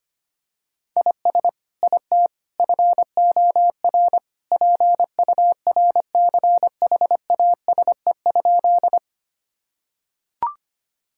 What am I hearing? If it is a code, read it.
Morse code, heard as a keyed tone, about 25 wpm, 700 Hz. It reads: IS IT FOR PURCHASE?
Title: Sentences from Top 1000 Words - Review Part 5 - 25wpm